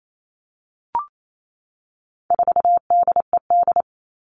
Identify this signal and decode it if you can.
Morse code, heard as a keyed tone, about 28 wpm, 700 Hz. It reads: 4DED
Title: All Cut Numbers - 4-Digit 28wpm